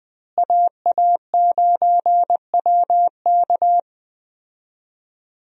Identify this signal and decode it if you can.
Morse code, heard as a keyed tone, about 21 wpm, 700 Hz. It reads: AA9WK
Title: USA Call Signs - 5 Characters - Repeated 3x - 20wpm